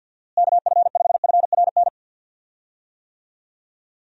Morse code, heard as a keyed tone, about 52 wpm, 700 Hz. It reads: KF5FRN
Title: USA Call Signs - 6 Characters - Repeated 2x - 50wpm